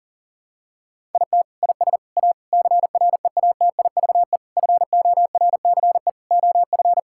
Morse code, heard as a keyed tone, about 40 wpm, 700 Hz. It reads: IT IS A CREATIVE FORCE OF
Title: Sentences from Top 1000 Words - Review Part 5 - 40wpm